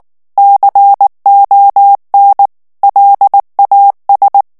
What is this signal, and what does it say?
Morse code, heard as a keyed tone, about 19 wpm, 800 Hz. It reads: CON LAS
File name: morse mp3 / ESPAGNOL TEXTE EN CLAIR 2 VITESSE 18 MOTS MINUTES